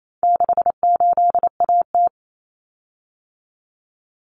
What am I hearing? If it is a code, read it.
Morse code, heard as a keyed tone, about 28 wpm, 700 Hz. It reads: 68AT
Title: Common Cut Numbers - 4-Digit 28wpm